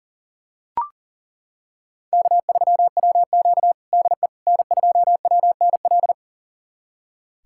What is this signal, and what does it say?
Morse code, heard as a keyed tone, about 40 wpm, 700 Hz. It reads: K3WQ DE N2WNL
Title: USA Callsign DE Callsign 40wpm